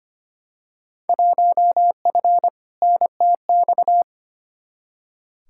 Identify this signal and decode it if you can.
Morse code, heard as a keyed tone, about 25 wpm, 700 Hz. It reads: 1F NTX
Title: Field Day Exchanges Encore - 25wpm